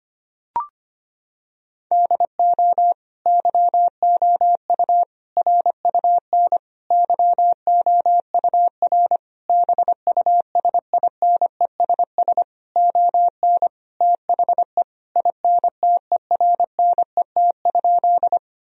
Morse code, heard as a keyed tone, about 25 wpm, 700 Hz. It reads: DO YOU RUN YOUR BUSINESS ON THE INTERNET?
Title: Sentences from Top 1000 Words - Review Part 4 - 25wpm